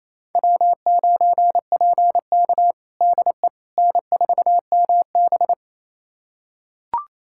Morse code, heard as a keyed tone, about 28 wpm, 700 Hz. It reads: W9PK DE N4MB
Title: USA Callsign DE Callsign - 4 Characters - 28wpm